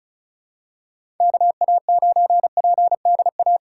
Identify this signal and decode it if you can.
Morse code, heard as a keyed tone, about 36 wpm, 700 Hz. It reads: KA9PDA
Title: USA Call Signs - 3x Character Spacing - 35wpm